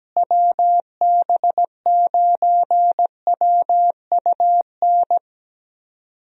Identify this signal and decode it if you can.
Morse code, heard as a keyed tone, about 17 wpm, 700 Hz. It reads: WB9WUN
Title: USA Call Signs - 6 Characters - Repeated 2x - 17wpm